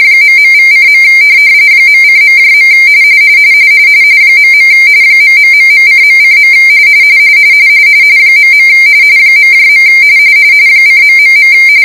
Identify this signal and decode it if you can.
RTTY, about 45.45 baud. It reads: ..=:. .:.. .. ....:.:. .. .. :...